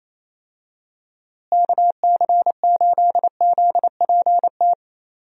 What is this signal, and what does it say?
Morse code, heard as a keyed tone, about 27 wpm, 700 Hz. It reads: KC8ZPT
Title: USA Call Signs - 6 Characters - Repeated 3x - 28wpm